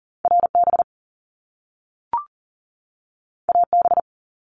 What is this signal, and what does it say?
Morse code, heard as a keyed tone, about 41 wpm, 700 Hz. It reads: RB AB
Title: Top Two Letter Combinations in Words 40wpm